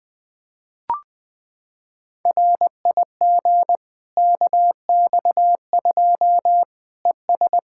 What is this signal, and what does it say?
Morse code, heard as a keyed tone, about 20 wpm, 700 Hz. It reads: RIG KX2 ES
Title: K1USN SST QSO Practice 20wpm